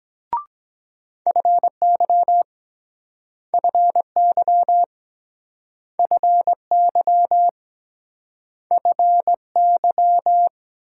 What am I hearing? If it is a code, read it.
Morse code, heard as a keyed tone, about 23 wpm, 700 Hz. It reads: FY FY FY FY